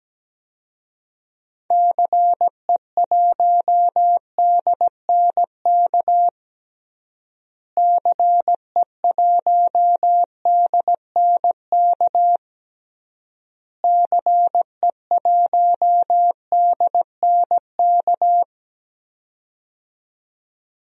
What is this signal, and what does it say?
Morse code, heard as a keyed tone, about 17 wpm, 700 Hz. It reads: CE1DNK CE1DNK CE1DNK
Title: DXCC-100 Call Signs - 6 Characters - Repeated 3x - 17wpm